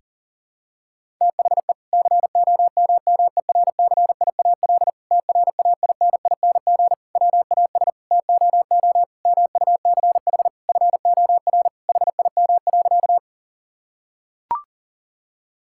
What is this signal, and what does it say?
Morse code, heard as a keyed tone, about 40 wpm, 700 Hz. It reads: THE COMMERCIAL TRAINING WAS TOO MUCH FOR HIM.
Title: Sentences from Top 1000 Words - Review Part 5 - 40wpm